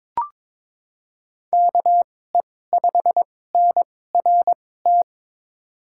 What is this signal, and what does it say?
Morse code, heard as a keyed tone, about 21 wpm, spaced as for 16 wpm, 700 Hz. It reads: KE5NRT
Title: USA Call Signs - 6 Characters - 2x Character Spacing - 22wpm